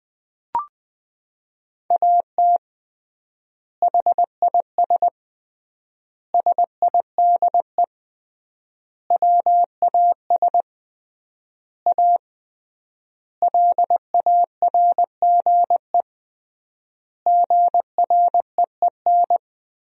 Morse code, heard as a keyed tone, about 21 wpm, 700 Hz. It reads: AT HIS SIDE WAS A LARGE GREEN
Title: 3x Word-Spacing - Sentences from Top 500 Words - 20wpm